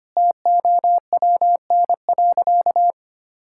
Morse code, heard as a keyed tone, about 25 wpm, 700 Hz. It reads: TOWN.
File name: practice / 25wpm / Sentences with 3-Word Phrases - 25wpm